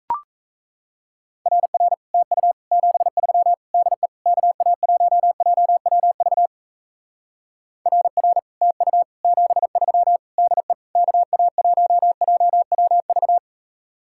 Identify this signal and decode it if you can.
Morse code, heard as a keyed tone, about 37 wpm, 700 Hz. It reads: RR TU 73 DE KA1JWV RR TU 73 DE KA1JWV